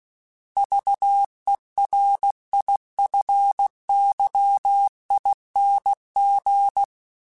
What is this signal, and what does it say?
Morse code, heard as a keyed tone, about 16 wpm, 800 Hz. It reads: VERIFYING
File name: Special INGs / Special INGs-mp3s / Sngl ing @15